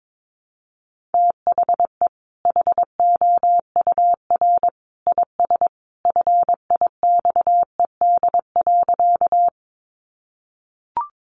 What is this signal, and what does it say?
Morse code, heard as a keyed tone, about 22 wpm, 700 Hz. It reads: THE HOUR IS FIXED.
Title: Sentences from Top 1100 Words 22wpm